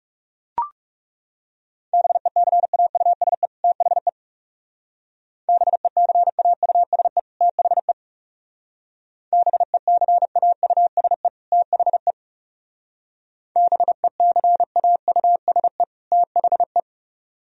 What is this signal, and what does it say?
Morse code, heard as a keyed tone, about 36 wpm, 700 Hz. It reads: BECAUSE THE BECAUSE THE BECAUSE THE BECAUSE THE
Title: Speed-Racing - Sets of 2 Words 30wpm